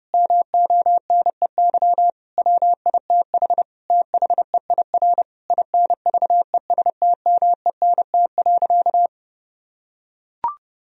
Morse code, heard as a keyed tone, about 30 wpm, 700 Hz. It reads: MONEY WITH THEIR INVESTMENT.
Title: Sentences from Top 1000 Words - Review Part 4 - 30wpm